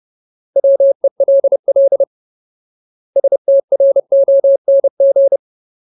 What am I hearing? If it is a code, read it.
Morse code, heard as a keyed tone, about 30 wpm, 550 Hz. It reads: WELL STRONG
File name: Ultimate Word List / 30WPM Ultimate